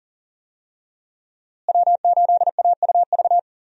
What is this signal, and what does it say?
Morse code, heard as a keyed tone, about 40 wpm, 700 Hz. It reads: W8AUV
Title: USA Call Signs - 5 Characters - Repeated 2x - 40wpm